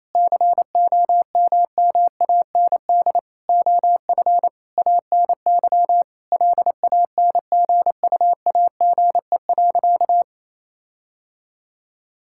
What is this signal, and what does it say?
Morse code, heard as a keyed tone, about 28 wpm, 700 Hz. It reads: COMMAND OF ANY LANGUAGE.
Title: Sentences from Top 1600 Words 28wpm